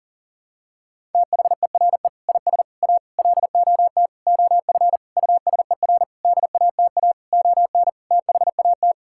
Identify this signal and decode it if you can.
Morse code, heard as a keyed tone, about 38 wpm, 700 Hz. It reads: THERE IS A LOT OF USER DATA ON THAT
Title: Sentences from Top 1000 Words - Review Part 6 - 40wpm